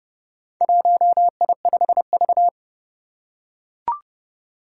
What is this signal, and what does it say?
Morse code, heard as a keyed tone, about 30 wpm, 700 Hz. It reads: 1I5V
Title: Four Challenging Letter-Number Combinations 30wpm